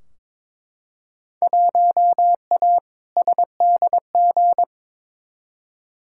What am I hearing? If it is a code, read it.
Morse code, heard as a keyed tone, about 21 wpm, 700 Hz. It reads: 1A SDG
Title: Field Day Exchanges Encore - 22wpm